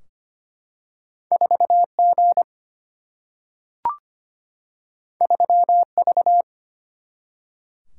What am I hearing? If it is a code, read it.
Morse code, heard as a keyed tone, about 25 wpm, 700 Hz. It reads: 4G 3V
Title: Single-Letter Single-Number 25wpm